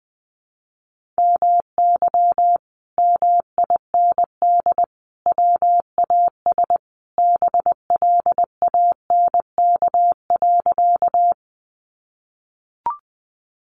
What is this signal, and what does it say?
Morse code, heard as a keyed tone, about 20 wpm, 700 Hz. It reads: MY MIND WAS BLANK.
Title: Sentences from Top 1500 Words 20wpm